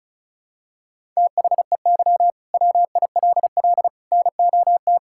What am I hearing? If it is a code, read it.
Morse code, heard as a keyed tone, about 36 wpm, 700 Hz. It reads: THEY WILL NOT
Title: Sentences from Top 1000 Words - Review Part 5 - 35wpm